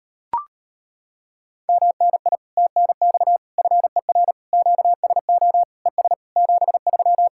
Morse code, heard as a keyed tone, about 33 wpm, 700 Hz. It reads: MNI TNX FER QSO ES 73